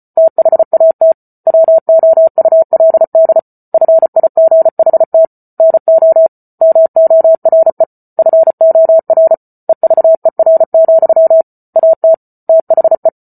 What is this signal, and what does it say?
Morse code, heard as a keyed tone, about 34 wpm, 650 Hz. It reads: THAT WOULD FIGHT NO MORE FOR EVER, AT THE
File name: Book Two / 08-Chapter 8 - 34 WPM